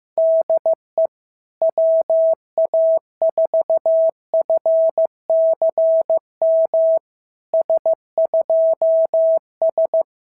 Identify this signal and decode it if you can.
Morse code, heard as a keyed tone, about 15 wpm, 690 Hz. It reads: DE WA4FCM S2S
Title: SOTA QSO Practice 15wpm